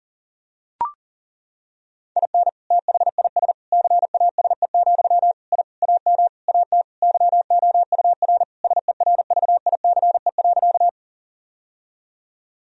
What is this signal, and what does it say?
Morse code, heard as a keyed tone, about 40 wpm, 700 Hz. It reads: IN THIS CASE, I AM AT YOUR SERVICE.
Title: Sentences from Top 1000 Words - Review Part 4 - 40wpm